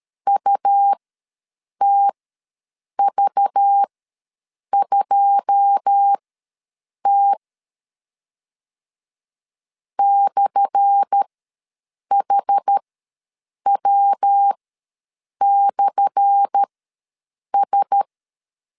Morse code, heard as a keyed tone, about 13 wpm, spaced as for 7 wpm, 800 Hz. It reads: UTV2T /HW/S
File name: LEZIONI / 10 SIGNS / 10A SLASH ws 12wpm 3443